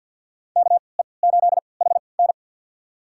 Morse code, heard as a keyed tone, about 51 wpm, spaced as for 28 wpm, 700 Hz. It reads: KE8HN
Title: USA Call Signs - 5 Characters - 3x Character Spacing - 50wpm